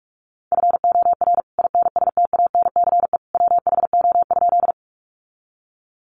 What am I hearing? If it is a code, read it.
Morse code, heard as a keyed tone, about 44 wpm, 700 Hz. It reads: FOR INSTANCE WHO?
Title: Sentences from Top 800 Words 45wpm